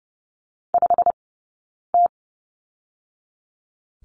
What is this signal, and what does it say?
Morse code, heard as a keyed tone, about 30 wpm, 700 Hz. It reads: 5 T
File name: Lesson 20d - Number 3 - Two Characters 30wpm